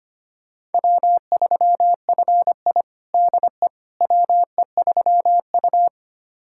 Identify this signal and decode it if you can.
Morse code, heard as a keyed tone, about 25 wpm, 700 Hz. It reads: W3FI DE WE3U
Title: USA Callsign DE Callsign - 4 Characters - 25wpm